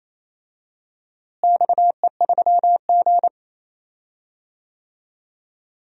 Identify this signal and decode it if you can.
Morse code, heard as a keyed tone, about 27 wpm, 700 Hz. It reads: XE3G
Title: DXCC-100 Call Signs - 4 Characters - Repeated 3x - 28wpm